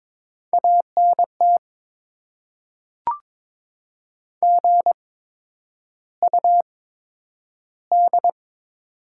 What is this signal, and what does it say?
Morse code, heard as a keyed tone, about 22 wpm, spaced as for 9 wpm, 700 Hz. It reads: ANT GUD
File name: Top 20 QSO Elements - 8x Character Spacing - 22wpm